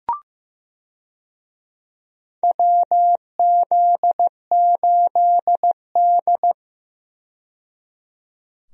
Morse code, heard as a keyed tone, about 15 wpm, 700 Hz. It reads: WZ8D
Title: USA Call Signs - 4 Characters - 15wpm